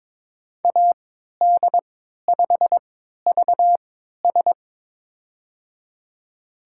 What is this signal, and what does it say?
Morse code, heard as a keyed tone, about 22 wpm, spaced as for 13 wpm, 700 Hz. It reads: AD5VS
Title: USA Call Signs - 5 Characters - 3x Character Spacing - 22wpm